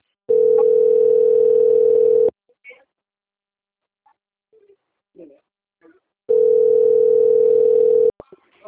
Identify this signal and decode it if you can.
Ringback tone (North America), 440 + 480 Hz, with speech.
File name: out- - - - - .62251